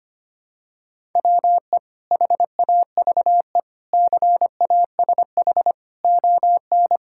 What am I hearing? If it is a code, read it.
Morse code, heard as a keyed tone, about 25 wpm, 700 Hz. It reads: WE HAVE CASH ON
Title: Sentences from Top 1000 Words - Review Part 5 - 25wpm